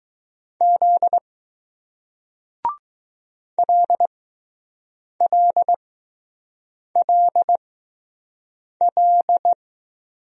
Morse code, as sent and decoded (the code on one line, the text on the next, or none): --.. / .-.. .-.. .-.. .-..
Z LLLL